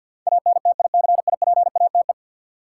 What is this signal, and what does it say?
Morse code, heard as a keyed tone, about 52 wpm, 700 Hz. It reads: ANTICIPATE
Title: Sentences from Top 1500 Words - New Word Drill - 50wpm